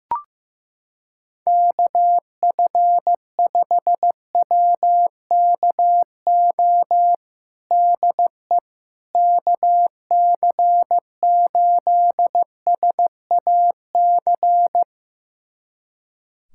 Morse code, heard as a keyed tone, about 15 wpm, 700 Hz. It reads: KF5WKO DE KC8SAC
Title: USA Callsign DE Callsign 15wpm